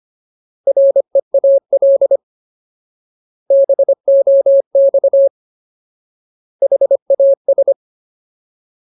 Morse code, heard as a keyed tone, about 25 wpm, 550 Hz. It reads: REAL BOX HAS
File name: Ultimate Word List / 25WPM Ultimate